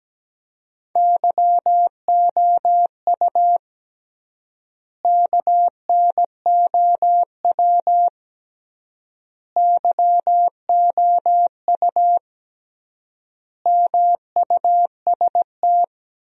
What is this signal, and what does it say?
Morse code, heard as a keyed tone, about 17 wpm, 700 Hz. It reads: YOU KNOW YOU MUST
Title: 3x Word-Spacing - Sentences from Top 500 Words - 17wpm